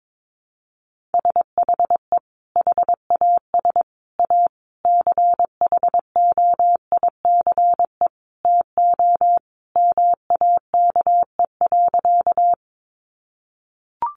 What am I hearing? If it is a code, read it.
Morse code, heard as a keyed tone, about 22 wpm, 700 Hz. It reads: SHE HAS A CHOICE TO MAKE.
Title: Sentences from Top 900 Words 22wpm